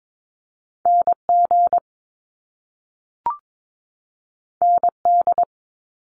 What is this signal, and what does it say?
Morse code, heard as a keyed tone, about 22 wpm, 700 Hz. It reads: NG ND
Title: Top Two Letter Combinations in Words 22wpm